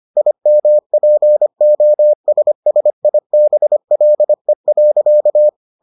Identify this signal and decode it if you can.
Morse code, heard as a keyed tone, about 25 wpm, 600 Hz. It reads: IMPOSSIBLE.